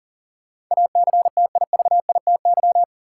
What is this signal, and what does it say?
Morse code, heard as a keyed tone, about 39 wpm, 700 Hz. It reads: ACTIVITY
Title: Sentences from Top 1000 Words - Review Part 4 - New Word Drill - 40wpm